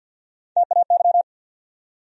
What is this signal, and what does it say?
Morse code, heard as a keyed tone, about 53 wpm, 700 Hz. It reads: TAY
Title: USA Call Sign Postfixes - 3 Letters 50wpm